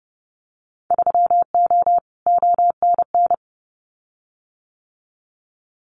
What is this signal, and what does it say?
Morse code, heard as a keyed tone, about 30 wpm, 700 Hz. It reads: 3O ONN
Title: Winter Field Day Exchanges 30wpm